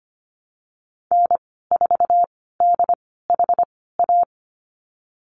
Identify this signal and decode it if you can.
Morse code, heard as a keyed tone, about 25 wpm, spaced as for 16 wpm, 700 Hz. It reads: N4DHA
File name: Lesson 15g - Number 5 - Call Signs 25wpm